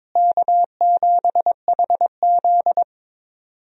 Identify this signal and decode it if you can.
Morse code, heard as a keyed tone, about 22 wpm, 700 Hz. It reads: K7HZ
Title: USA Call Signs - 4 Characters - Repeated 3x - 22wpm